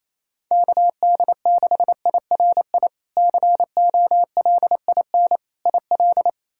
Morse code, heard as a keyed tone, about 28 wpm, 700 Hz. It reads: KD6IRI COLIN IL
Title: K1USN SST QSO Practice - 28wpm